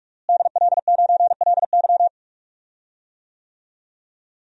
Morse code, heard as a keyed tone, about 43 wpm, 700 Hz. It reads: DL9LY
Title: DXCC-100 Call Signs - 5 Characters - Repeated 2x - 45wpm